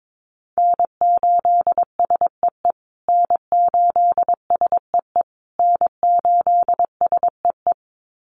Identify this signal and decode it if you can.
Morse code, heard as a keyed tone, about 22 wpm, 700 Hz. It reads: N8SEE N8SEE N8SEE
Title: USA Call Signs Repeated 3x 22wpm